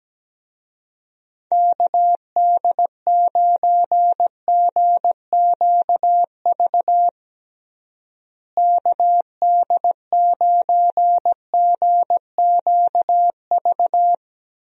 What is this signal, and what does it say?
Morse code, heard as a keyed tone, about 17 wpm, 700 Hz. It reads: KD9GQV KD9GQV
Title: USA Call Signs - 6 Characters - Repeated 3x - 17wpm